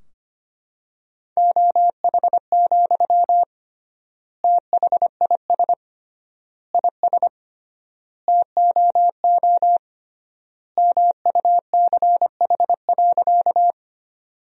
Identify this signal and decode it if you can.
Morse code, heard as a keyed tone, about 25 wpm, 700 Hz. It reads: OH, THIS IS TOO MUCH.
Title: 3x Word-Spacing - Sentences from Top 500 Words - 25wpm